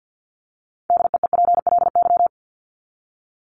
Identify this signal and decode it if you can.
Morse code, heard as a keyed tone, about 48 wpm, 700 Hz. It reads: DEEPLY
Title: LY Suffix - Top 100 Words with Base Words 50wpm